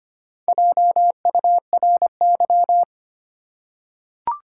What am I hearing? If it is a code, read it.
Morse code, heard as a keyed tone, about 25 wpm, 700 Hz. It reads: JURY
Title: Four Letter Words - 25wpm